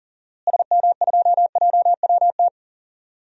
Morse code, heard as a keyed tone, about 40 wpm, 700 Hz. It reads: SM2JWT
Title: DXCC-100 Call Signs - 6 Characters - Repeated 2x - 40wpm